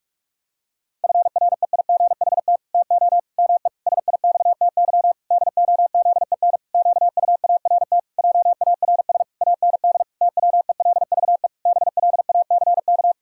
Morse code, heard as a keyed tone, about 46 wpm, 700 Hz. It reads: FREIGHT TO ME SIXTY DOZEN QUART JARS AND TWELVE BLACK
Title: Pangram Challenge v2 - 45wpm